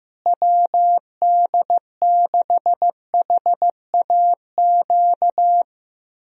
Morse code, heard as a keyed tone, about 16 wpm, 700 Hz. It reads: WD6HAQ
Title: USA Call Signs - 6 Characters - Repeated 2x - 15wpm